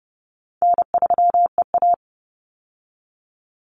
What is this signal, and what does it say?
Morse code, heard as a keyed tone, about 30 wpm, 700 Hz. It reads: N3EA
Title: Easy Call Signs Repeated 2x - 30wpm